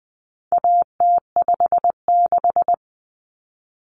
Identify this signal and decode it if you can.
Morse code, heard as a keyed tone, about 20 wpm, 700 Hz. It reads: AT56
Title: Common Cut Numbers - 4-Digit 20wpm